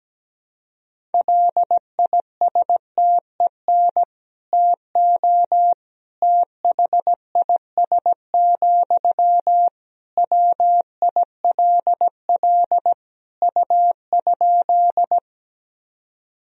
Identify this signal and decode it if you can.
Morse code, heard as a keyed tone, about 17 wpm, 700 Hz. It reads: LISTEN TO THIS, WILL U?
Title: Sentences from Top 400 Words and Abbreviations 17wpm